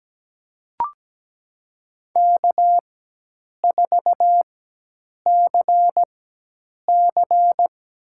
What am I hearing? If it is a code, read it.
Morse code, heard as a keyed tone, about 17 wpm, spaced as for 8 wpm, 700 Hz. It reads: K4CC
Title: USA Call Signs - 4x Character Spacing - 17wpm